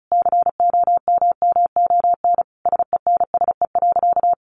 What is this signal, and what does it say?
Morse code, heard as a keyed tone, about 34 wpm, 700 Hz. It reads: COMMON SENSE.
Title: Sentences from Top 1500 Words 35wpm